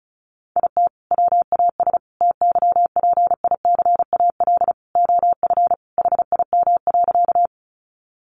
Morse code, heard as a keyed tone, about 34 wpm, 700 Hz. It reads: IT WAS TYPICAL OF HIM.
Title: Sentences from Top 1100 Words 35wpm